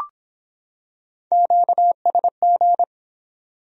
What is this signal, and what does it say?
Morse code, heard as a keyed tone, about 25 wpm, 700 Hz. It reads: QSG